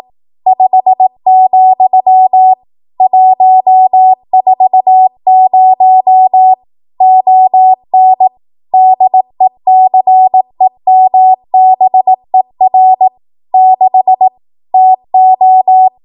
Morse code, heard as a keyed tone, about 18 wpm, 750 Hz. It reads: 5, 140 ON DECEMBER 6 TO
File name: Archive / 18 WPM / 18WPM